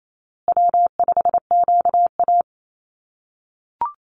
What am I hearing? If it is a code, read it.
Morse code, heard as a keyed tone, about 28 wpm, 700 Hz. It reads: W5QA
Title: USA Call Signs Repeated 2x 28wpm